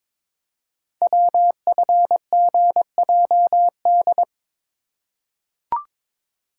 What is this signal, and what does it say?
Morse code, heard as a keyed tone, about 22 wpm, 700 Hz. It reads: WFGJD
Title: Code Groups - 5 Letter:Numbers - 22wpm